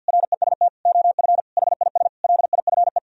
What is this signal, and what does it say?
Morse code, heard as a keyed tone, about 64 wpm, 700 Hz. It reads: REST OF HIS LIFE